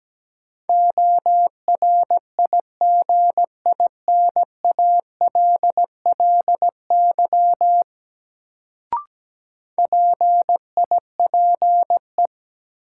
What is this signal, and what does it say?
Morse code, heard as a keyed tone, about 17 wpm, 700 Hz. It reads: ORIGINALLY PIPE
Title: Sentences from Top 1500 Words - New Word Drill Alt - 17wpm